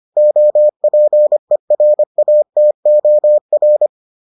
Morse code, heard as a keyed tone, about 25 wpm, 600 Hz. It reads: OPERATOR